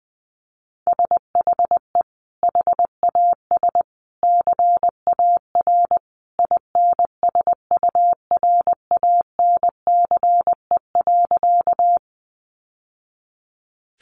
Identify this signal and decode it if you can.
Morse code, heard as a keyed tone, about 19 wpm, 700 Hz. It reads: SHE HAS CAR INSURANCE.
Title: Sentences from Top 1100 Words 20wpm